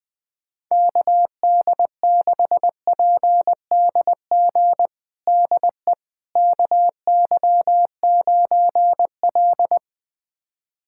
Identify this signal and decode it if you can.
Morse code, heard as a keyed tone, about 21 wpm, 700 Hz. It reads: KD6PDG DE KY9L
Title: USA Callsign DE Callsign 20wpm